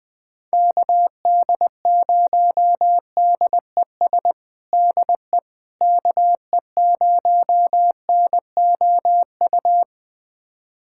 Morse code, heard as a keyed tone, about 20 wpm, 700 Hz. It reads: KD0DES DE KE0NOU
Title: USA Callsign DE Callsign - 6 Characters - 20wpm